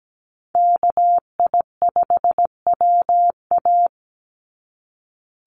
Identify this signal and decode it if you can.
Morse code, heard as a keyed tone, about 17 wpm, 700 Hz. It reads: KI5WA
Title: NA SOTA Chaser Callsigns 17wpm